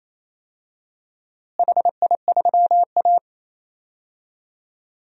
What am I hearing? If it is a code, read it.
Morse code, heard as a keyed tone, about 28 wpm, 700 Hz. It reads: HI3A
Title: DXCC-100 Call Signs - 4 Characters - Repeated 2x - 28wpm